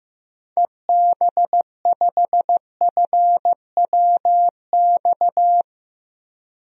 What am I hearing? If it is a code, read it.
Morse code, heard as a keyed tone, about 15 wpm, 700 Hz. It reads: EB5FWX
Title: DXCC-100 Call Signs - 6 Characters - 15wpm